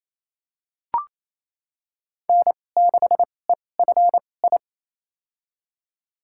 Morse code, heard as a keyed tone, about 28 wpm, spaced as for 20 wpm, 700 Hz. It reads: N6EFI
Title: USA Call Signs - 5 Characters - 2x Character Spacing - 28wpm